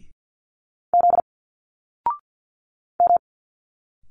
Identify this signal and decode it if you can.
Morse code, heard as a keyed tone, about 50 wpm, 700 Hz. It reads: Z M
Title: Single Letters 50wpm